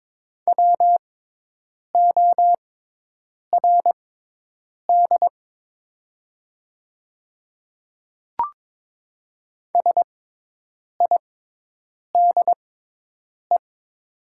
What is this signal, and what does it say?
Morse code, heard as a keyed tone, about 21 wpm, spaced as for 8 wpm, 700 Hz. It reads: WORD SIDE
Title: 6x Character Spacing - Top 100 Words - 22wpm